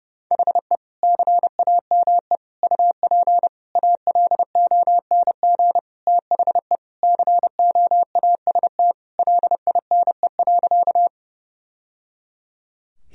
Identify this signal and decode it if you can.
Morse code, heard as a keyed tone, about 32 wpm, 700 Hz. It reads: HE CAME UP ALONG THE COAST LINE.
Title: Sentences from Top 1000 Words - Review Part 4 - 30wpm